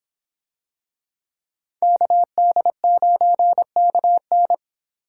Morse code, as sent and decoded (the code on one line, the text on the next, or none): -.- -.. ----. -.- -.
KD9KN